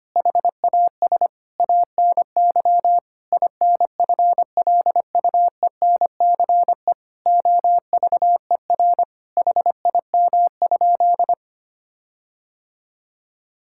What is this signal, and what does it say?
Morse code, heard as a keyed tone, about 25 wpm, 700 Hz. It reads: HAS ANY INFLUENCE OVER HIM?
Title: Sentences from Top 1600 Words 25wpm